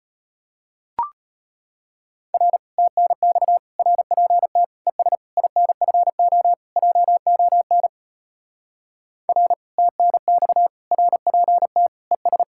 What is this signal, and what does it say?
Morse code, heard as a keyed tone, about 38 wpm, 700 Hz. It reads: R TNX RPT ES INFO JON R TNX RPT ES